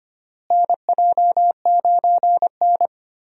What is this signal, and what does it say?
Morse code, heard as a keyed tone, about 25 wpm, 700 Hz. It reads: NJ9N
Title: USA Call Signs - 4 Characters - Repeated 2x - 25wpm